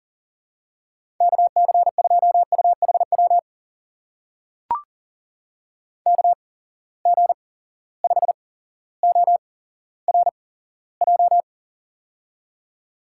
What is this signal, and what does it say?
Morse code, heard as a keyed tone, about 39 wpm, spaced as for 17 wpm, 700 Hz. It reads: KC2UHW KG5ORJ
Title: USA Call Signs - 6 Characters - 8x Character Spacing - 40wpm